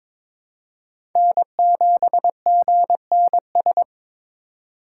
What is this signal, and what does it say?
Morse code, heard as a keyed tone, about 22 wpm, 700 Hz. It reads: N7GNS
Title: USA Call Signs - 5 Characters - 3x Character Spacing - 22wpm